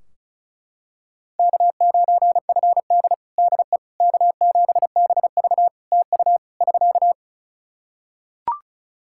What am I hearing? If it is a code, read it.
Morse code, heard as a keyed tone, about 36 wpm, 700 Hz. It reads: K9FD DE K7BV TU <SK>
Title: Practice QSO - Exchange 4 - Ending - 35wpm